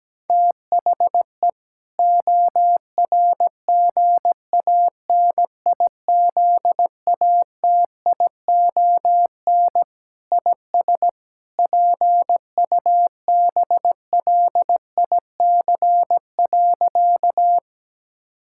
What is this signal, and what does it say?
Morse code, heard as a keyed tone, about 17 wpm, 700 Hz. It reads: THE ORGANIZATION IS PUBLIC.
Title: Sentences from Top 1000 Words - Review Part 5 - 17wpm